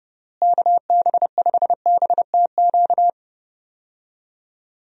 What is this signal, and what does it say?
Morse code, heard as a keyed tone, about 30 wpm, 700 Hz. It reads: KB5BTQ
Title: USA Call Signs - 6 Characters - Repeated 3x - 30wpm